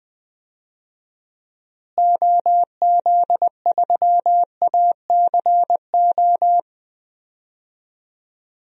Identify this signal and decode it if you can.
Morse code, heard as a keyed tone, about 20 wpm, 700 Hz. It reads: OZ3ACO
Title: DXCC-100 Call Signs - 6 Characters - Repeated 3x - 20wpm